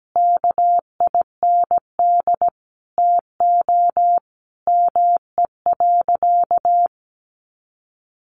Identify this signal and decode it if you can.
Morse code, heard as a keyed tone, about 17 wpm, 700 Hz. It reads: KIND TO ME.
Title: Sentences from Top 200 Words 17wpm